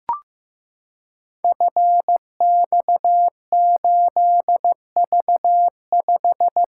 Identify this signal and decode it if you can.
Morse code, heard as a keyed tone, about 15 wpm, 700 Hz. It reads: FX8V5
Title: Code Groups - 5 Letter:Numbers - 15wpm